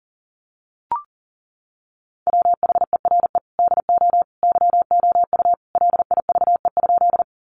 Morse code, heard as a keyed tone, about 40 wpm, 700 Hz. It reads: WHERE DO YOU LIVE?
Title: Sentences from Top 200 Words 40wpm